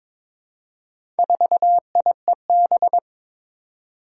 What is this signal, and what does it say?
Morse code, heard as a keyed tone, about 22 wpm, 700 Hz. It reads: 4IEB
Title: Four Challenging Letter-Number Combinations 22wpm